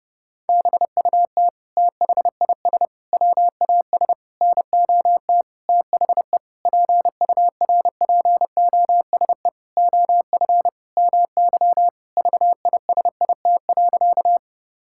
Morse code, heard as a keyed tone, about 30 wpm, 700 Hz. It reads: BUT THIS WAS NOT THE PURPOSE OF MY VISIT.
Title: Sentences from Top 1000 Words - Review Part 6 - 30wpm